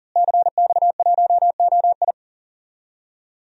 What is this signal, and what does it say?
Morse code, heard as a keyed tone, about 40 wpm, 700 Hz. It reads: CX1OI
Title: DXCC-100 Call Signs - 5 Characters - Repeated 2x - 40wpm